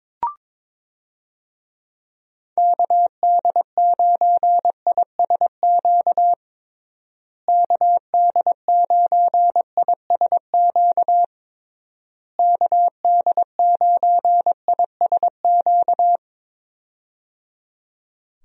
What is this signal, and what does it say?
Morse code, heard as a keyed tone, about 22 wpm, 700 Hz. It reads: KD9ISQ KD9ISQ KD9ISQ
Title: USA Call Signs - 6 Characters - Repeated 3x - 22wpm